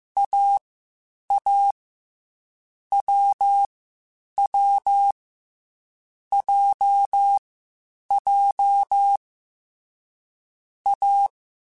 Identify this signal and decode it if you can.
Morse code, heard as a keyed tone, about 15 wpm, 800 Hz. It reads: A A W W J J A
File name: Difficult Combos / Difficult Combos-mp3s / AWJ1 @13